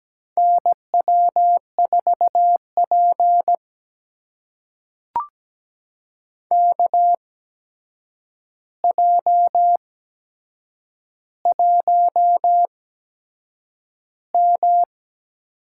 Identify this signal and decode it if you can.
Morse code, heard as a keyed tone, about 17 wpm, spaced as for 7 wpm, 700 Hz. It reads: NW4P KJ1M
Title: USA Call Signs - 4 Characters - 8x Character Spacing - 17wpm